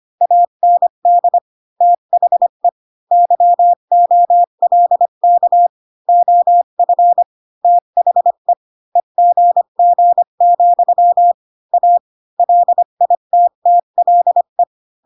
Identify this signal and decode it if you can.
Morse code, heard as a keyed tone, about 25 wpm, 700 Hz. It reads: AND THE YOLK OF THE EGG, A LITTLE
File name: Pinocchio / Pinocchio ch5 20 25 ik1hge